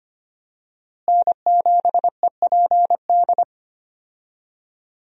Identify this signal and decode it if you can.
Morse code, heard as a keyed tone, about 25 wpm, 700 Hz. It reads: N7EPD
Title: USA Call Signs - 5 Characters - Repeated 2x - 25wpm